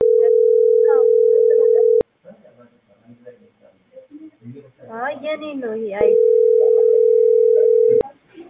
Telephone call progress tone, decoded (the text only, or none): ringback tone